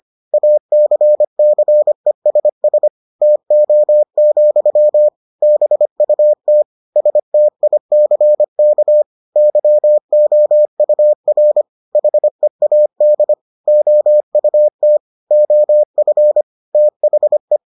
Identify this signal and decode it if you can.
Morse code, heard as a keyed tone, about 25 wpm, 600 Hz. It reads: ACCESS TO, BUT STICK YOUR HEAD OUT OF THE